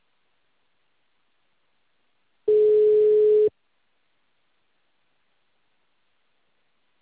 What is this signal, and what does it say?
Ringback tone (Europe, 425 Hz).
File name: out- -2044- - - .68877